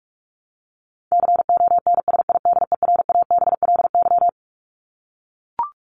Morse code, heard as a keyed tone, about 44 wpm, 700 Hz. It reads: CONSIDERABLY
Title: LY Suffix - Top 100 Words with Base Words 45wpm